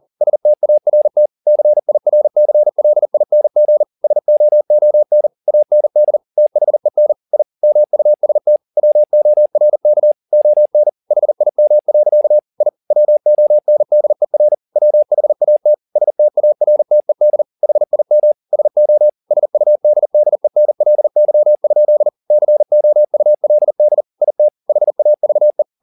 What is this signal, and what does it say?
Morse code, heard as a keyed tone, about 40 wpm, 600 Hz. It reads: START CIRCLING SOON AND THEN I MUST WORK ON HIM. I WONDER WHAT STARTED HIM SO SUDDENLY? COULD IT HAVE
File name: the old man and the sea 2 40wpm